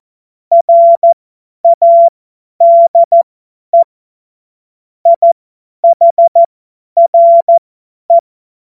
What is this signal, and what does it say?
Morse code, heard as a keyed tone, about 14 wpm, spaced as for 10 wpm, 680 Hz. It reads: RADE IHRE
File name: Morsekurs / Witze / Witz 02. Februar 2021, 50 F70